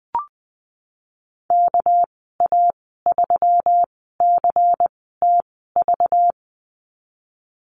Morse code, heard as a keyed tone, about 19 wpm, 700 Hz. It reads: K A 3 C T V
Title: 231 Lesson 37g - Letter x - Call Signs 20wpm